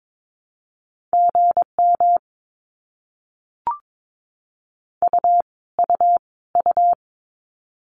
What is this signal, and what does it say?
Morse code, heard as a keyed tone, about 22 wpm, 700 Hz. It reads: GM U U U
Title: Abbreviations Repeated 3x - 22wpm